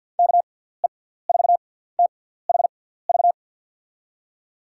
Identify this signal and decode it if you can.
Morse code, heard as a keyed tone, about 52 wpm, spaced as for 17 wpm, 700 Hz. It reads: KE4THV
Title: USA Call Signs - 6 Characters - 6x Character Spacing - 50wpm